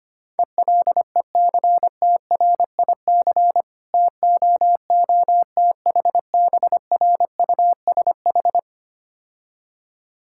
Morse code, heard as a keyed tone, about 25 wpm, 700 Hz. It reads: ELECTRIC TOOTHBRUSH
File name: practice / 25wpm / Difficult Words - Speed-Racing with Hints - 25wpm